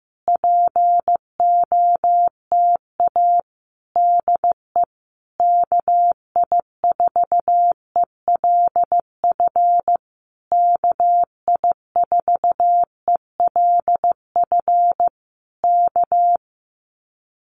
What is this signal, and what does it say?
Morse code, heard as a keyed tone, about 15 wpm, 700 Hz. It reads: POTA DE KI4ELF KI4ELF K
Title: Common QSO Phrases Encore 15wpm